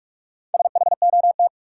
Morse code, heard as a keyed tone, about 47 wpm, 700 Hz. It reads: SHOT
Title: Sentences from Top 1100 Words - New Word Drill Alt - 45wpm